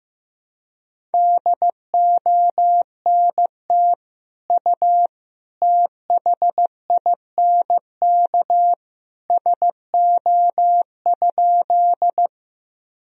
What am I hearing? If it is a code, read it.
Morse code, heard as a keyed tone, about 15 wpm, 700 Hz. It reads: DONT U THINK SO?
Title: Sentences from Top 300 Words and Abbreviations 15wpm